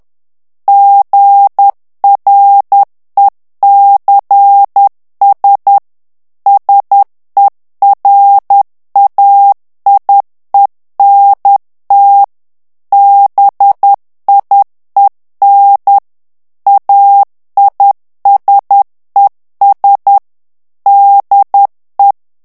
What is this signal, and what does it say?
Morse code, heard as a keyed tone, about 10 wpm, 800 Hz. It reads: GRECS SERAIENT BIEN AISES DE
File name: CW / FICHIERS-MP3 / FRANCAIS TEXTE CLAIR 2 VITESSE 10 MOTS MINUTES